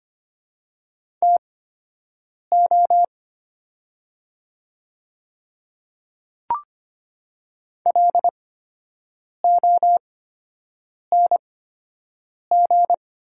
Morse code, heard as a keyed tone, about 25 wpm, spaced as for 7 wpm, 700 Hz. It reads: TO LONG
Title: 8x Character Spacing - Top 100 Words - 25wpm